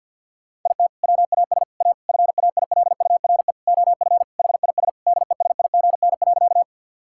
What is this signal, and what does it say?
Morse code, heard as a keyed tone, about 50 wpm, 700 Hz. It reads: IT WAS A FAILURE OF HIS DESIGN.
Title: Sentences from Top 1000 Words - Review Part 6 - 50wpm